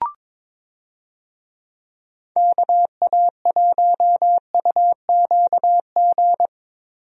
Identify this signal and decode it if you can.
Morse code, heard as a keyed tone, about 22 wpm, 700 Hz. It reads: KA1UQG
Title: USA Call Signs - 6 Characters - Repeated 3x - 22wpm